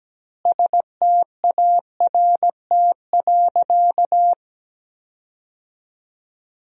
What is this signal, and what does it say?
Morse code, heard as a keyed tone, about 17 wpm, 700 Hz. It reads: START.
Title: 2x Word-Spacing - Sentences from Top 300 Words - 17wpm